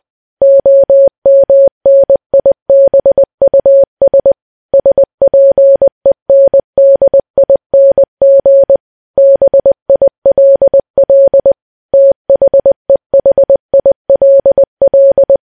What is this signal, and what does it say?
Morse code, heard as a keyed tone, about 20 wpm, 550 Hz. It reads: OMNIBUS SPENDING BILL THEHILL
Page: news-headlines-in-morse-code-at-20-wpm